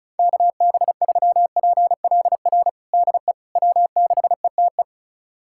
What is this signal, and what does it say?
Morse code, heard as a keyed tone, about 36 wpm, 700 Hz. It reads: KB3PLR DE W6ETE
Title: USA Callsign DE Callsign 35wpm